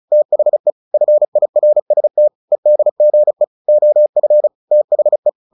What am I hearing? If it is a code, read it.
Morse code, heard as a keyed tone, about 34 wpm, 600 Hz. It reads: THE FIRST EDGE OF THE
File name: the old man and the sea / the old man and the sea 1 35wpm